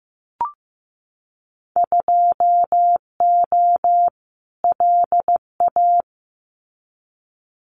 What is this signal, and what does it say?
Morse code, heard as a keyed tone, about 15 wpm, 700 Hz. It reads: 2O LA
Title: Winter Field Day Exchanges 15wpm